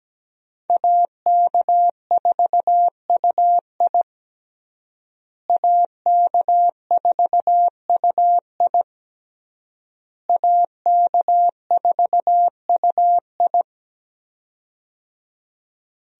Morse code, heard as a keyed tone, about 17 wpm, 700 Hz. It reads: AK4UI AK4UI AK4UI
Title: USA Call Signs - 5 Characters - Repeated 3x - 17wpm